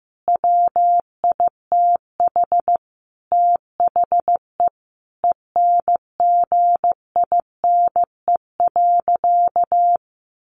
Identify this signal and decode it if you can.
Morse code, heard as a keyed tone, about 15 wpm, 700 Hz. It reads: WITH THE ENGINE.
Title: Sentences from Top 1500 Words 15wpm